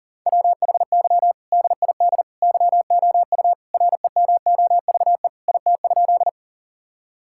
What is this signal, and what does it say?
Morse code, heard as a keyed tone, about 37 wpm, 700 Hz. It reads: WHY DID YOU REMOVE IT?
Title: Sentences from Top 1000 Words - Review Part 5 - 40wpm